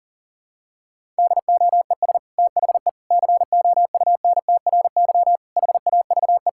Morse code, heard as a keyed tone, about 40 wpm, 700 Hz. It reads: DOES THE COUNTRY HAVE
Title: Sentences from Top 1000 Words - Review Part 5 - 40wpm